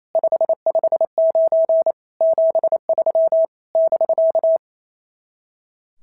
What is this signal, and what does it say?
Morse code, heard as a keyed tone, about 25 wpm, 650 Hz. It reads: 559 73 <BK>
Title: SOTA QSO Practice 28wpm